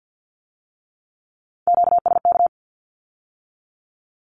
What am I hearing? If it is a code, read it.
Morse code, heard as a keyed tone, about 50 wpm, 700 Hz. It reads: QSK
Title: Q-Codes with Spoken Meaning - 50wpm